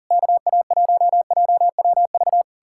Morse code, heard as a keyed tone, about 52 wpm, 700 Hz. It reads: KA1JWV